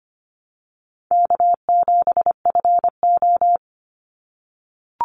Morse code, heard as a keyed tone, about 25 wpm, 700 Hz. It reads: K7FO
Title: NA SOTA Chaser Callsigns 25wpm